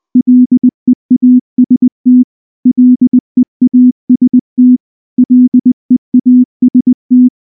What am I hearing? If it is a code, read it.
Morse code, heard as a keyed tone, about 19 wpm, 250 Hz. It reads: LEAST LEAST LEAST
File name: morse-training-20WPM